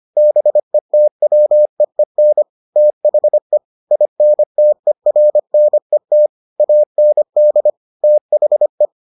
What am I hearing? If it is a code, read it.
Morse code, heard as a keyed tone, about 25 wpm, 600 Hz. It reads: BETWEEN THE INTERNET AND THE